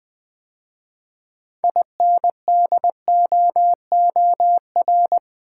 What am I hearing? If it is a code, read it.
Morse code, heard as a keyed tone, about 20 wpm, 700 Hz. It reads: INDOOR
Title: 6 Letter Words - First 3 Letters Given - 20wpm